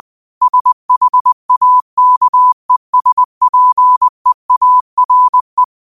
Morse code, heard as a keyed tone, about 20 wpm, 1000 Hz. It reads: SHAKESPEARE
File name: 20wpm